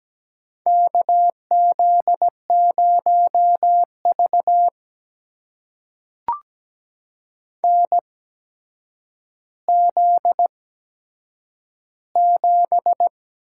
Morse code, heard as a keyed tone, about 17 wpm, spaced as for 8 wpm, 700 Hz. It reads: KZ0V NZ7
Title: USA Call Signs - 4 Characters - 8x Character Spacing - 17wpm